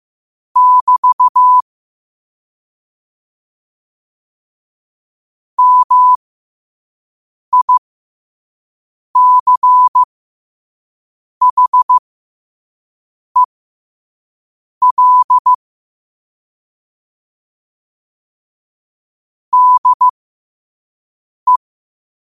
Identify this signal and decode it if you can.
Morse code, heard as a keyed tone, about 15 wpm, spaced as for 5 wpm, 1000 Hz. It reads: <BT> MICHEL DE
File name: qotdfiles / 5wpm